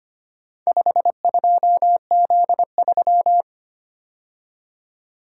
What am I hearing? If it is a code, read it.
Morse code, heard as a keyed tone, about 25 wpm, 700 Hz. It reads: 52Z3
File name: practice / 25wpm / Four Letter-Number Combinations - 25wpm